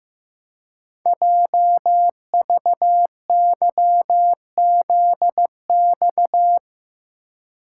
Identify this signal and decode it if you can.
Morse code, heard as a keyed tone, about 15 wpm, 700 Hz. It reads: JVYZX
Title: Code Groups - 5 Letters - 15wpm